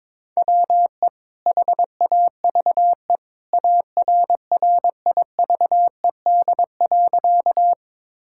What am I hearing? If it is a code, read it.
Morse code, heard as a keyed tone, about 22 wpm, 700 Hz. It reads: WE HAVE ARRIVED.
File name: practice / 22wpm / Sentences with 2-Word Phrases - 22wpm